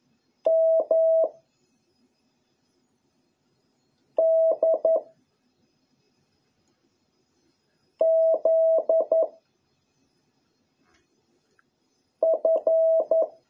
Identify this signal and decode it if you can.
Morse code, heard as a keyed tone, about 11 wpm, 650 Hz. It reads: M D Z F